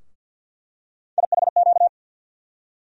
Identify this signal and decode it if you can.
Morse code, heard as a keyed tone, about 49 wpm, 700 Hz. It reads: IHQ
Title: USA Call Sign Postfixes - 3 Letters 50wpm